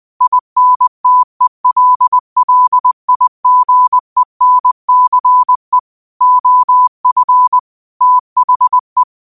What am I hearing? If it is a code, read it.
Morse code, heard as a keyed tone, about 20 wpm, 1000 Hz. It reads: INTELLIGENCE OF THE
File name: 20wpm